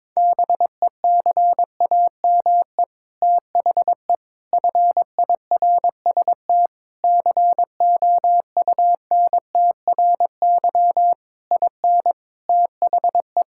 Morse code, heard as a keyed tone, about 22 wpm, 700 Hz. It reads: BECAME THE FIRST COUNTRY IN THE
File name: practice / 22wpm / Sets of 8 Words - 22wpm